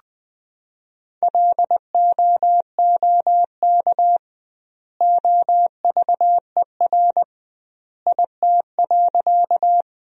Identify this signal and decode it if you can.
Morse code, heard as a keyed tone, about 21 wpm, 700 Hz. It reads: LOOK OVER IT.
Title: 2x Word-Spacing - Sentences from Top 100 Words - 20wpm